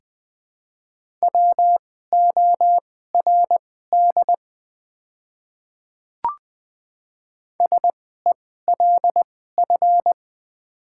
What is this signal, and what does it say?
Morse code, heard as a keyed tone, about 20 wpm, spaced as for 14 wpm, 700 Hz. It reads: WORD SELF
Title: 2x Character Spacing - Top 200 Words - 20wpm